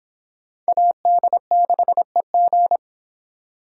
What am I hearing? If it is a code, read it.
Morse code, heard as a keyed tone, about 25 wpm, 700 Hz. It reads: AD6EG